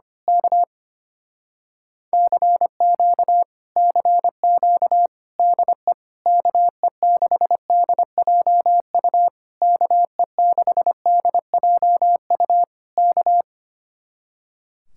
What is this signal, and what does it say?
Morse code, heard as a keyed tone, about 24 wpm, 700 Hz. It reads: K CQ CQ DE KE6DJU KE6DJU K
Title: Speed-Racing Common QSO Phrases - 25wpm